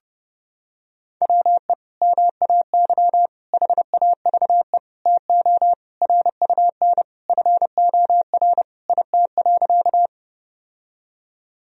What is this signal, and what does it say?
Morse code, heard as a keyed tone, about 32 wpm, 700 Hz. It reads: WE MAY HAVE TO RUN FOR IT.
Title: Sentences from Top 1000 Words - Review Part 1 - 30wpm